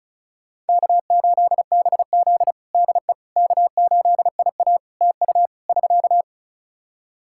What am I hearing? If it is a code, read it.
Morse code, heard as a keyed tone, about 34 wpm, 700 Hz. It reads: K8BZ DE K8IA TU <SK>
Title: K1USN SST QSO Practice 35wpm